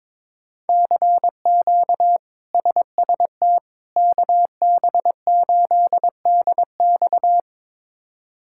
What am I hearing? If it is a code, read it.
Morse code, heard as a keyed tone, about 22 wpm, 700 Hz. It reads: CQ SST KB8DX
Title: K1USN SST QSO Practice - 22wpm